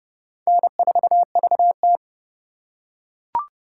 Morse code, heard as a keyed tone, about 27 wpm, 700 Hz. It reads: N4VT